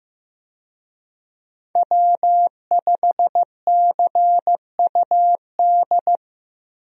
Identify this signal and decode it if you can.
Morse code, heard as a keyed tone, about 15 wpm, 700 Hz. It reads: W5CUD
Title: USA Call Signs - 5 Characters - Repeated 2x - 15wpm